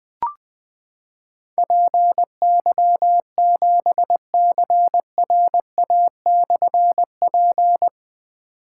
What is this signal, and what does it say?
Morse code, heard as a keyed tone, about 20 wpm, 700 Hz. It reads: PY7CRA/P
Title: Difficult Call Signs 20wpm